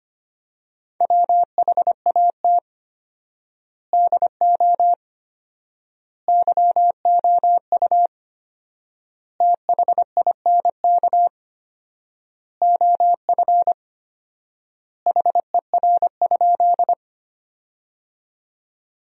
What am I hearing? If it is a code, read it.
Morse code, heard as a keyed tone, about 25 wpm, 700 Hz. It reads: WHAT DO YOU THINK OF HER?
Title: 4x Word-Spacing - Sentences from Top 200 Words - 25wpm